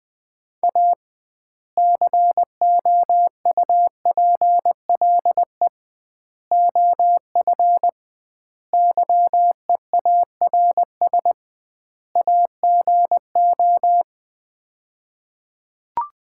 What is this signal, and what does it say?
Morse code, heard as a keyed tone, about 20 wpm, 700 Hz. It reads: A COUPLE OF YEARS AGO
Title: 2x Word-Spacing - Sets of 5 Words - 20wpm